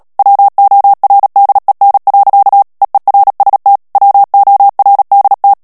Morse code, heard as a keyed tone, about 37 wpm, 800 Hz. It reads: WORDEN. EERST WORDT
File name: morse mp3 / HOLLANDAIS texte clair 2 vitesse 35 mots minute